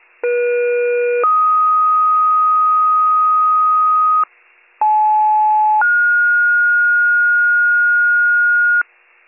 Two-tone page, heard as a one-second tone, then a three-second tone, about 480 then 1250 Hz, about 860 then 1430 Hz, with noise. It is heard from 0.2 to 8.8 s.